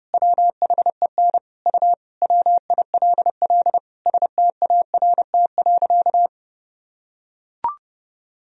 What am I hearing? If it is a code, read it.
Morse code, heard as a keyed tone, about 32 wpm, 700 Hz. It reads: WHEN U WILL START.
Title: Sentences from Top 300 Words and Abbreviations 30wpm